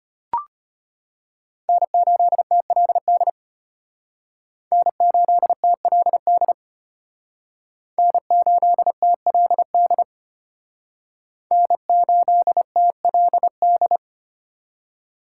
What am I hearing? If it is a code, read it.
Morse code, heard as a keyed tone, about 32 wpm, 700 Hz. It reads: N8TLD N8TLD N8TLD N8TLD